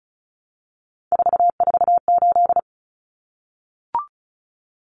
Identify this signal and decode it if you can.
Morse code, heard as a keyed tone, about 34 wpm, 700 Hz. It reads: 448
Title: Common Cut Numbers - 3-Digit 35wpm